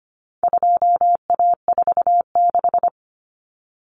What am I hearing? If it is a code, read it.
Morse code, heard as a keyed tone, about 25 wpm, 700 Hz. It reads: 2A46
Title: Common Cut Numbers - 4-Digit 25wpm